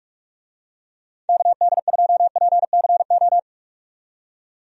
Morse code, heard as a keyed tone, about 41 wpm, 700 Hz. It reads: KD2PCO
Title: USA Call Signs - 6 Characters - 45wpm